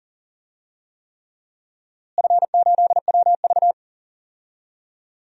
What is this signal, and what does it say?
Morse code, heard as a keyed tone, about 40 wpm, 700 Hz. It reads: F8WV
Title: DXCC-100 Call Signs - 4 Characters - Repeated 3x - 40wpm